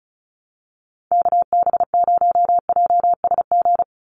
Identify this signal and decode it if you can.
Morse code, heard as a keyed tone, about 34 wpm, 700 Hz. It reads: KB0JSG
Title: USA Call Signs 35wpm